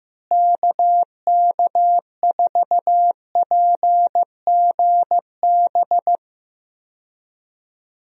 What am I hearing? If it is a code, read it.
Morse code, heard as a keyed tone, about 15 wpm, 700 Hz. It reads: KK4PGB
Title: USA Call Signs - 6 Characters - Repeated 2x - 15wpm